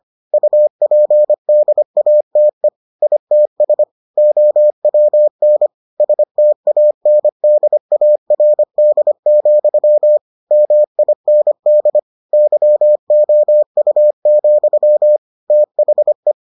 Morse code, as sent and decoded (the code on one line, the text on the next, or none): ..- .--. -.. .- - . / .. - ... / --- .-- -. / ... - .- -. -.. .- .-. -.. --..-- / -- .. -. -.. / -.-- --- ..- --..-- / - .... .
UPDATE ITS OWN STANDARD, MIND YOU, THE